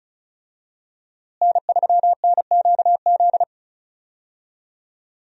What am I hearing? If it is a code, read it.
Morse code, heard as a keyed tone, about 34 wpm, 700 Hz. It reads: N3NQZ
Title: USA Call Signs - 5 Characters - Repeated 3x - 35wpm